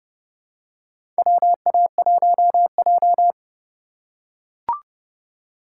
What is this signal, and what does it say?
Morse code, heard as a keyed tone, about 30 wpm, 700 Hz. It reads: WA1J
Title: USA Call Signs - 4 Characters - 3x Character Spacing - 30wpm